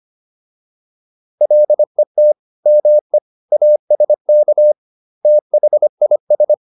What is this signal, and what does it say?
Morse code, heard as a keyed tone, about 25 wpm, 600 Hz. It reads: LET ME ASK THIS